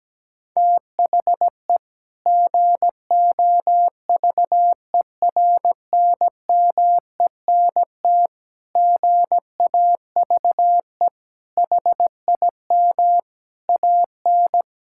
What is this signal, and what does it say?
Morse code, heard as a keyed tone, about 17 wpm, 700 Hz. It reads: THE GOVERNMENT GAVE HIM AN
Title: Sentences from Top 1000 Words - Review Part 5 - 17wpm